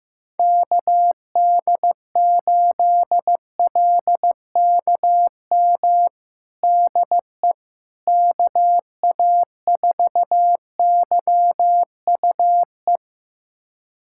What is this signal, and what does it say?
Morse code, heard as a keyed tone, about 16 wpm, 700 Hz. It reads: KD8LKM DE KA4YUE
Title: USA Callsign DE Callsign - 6 Characters - 15wpm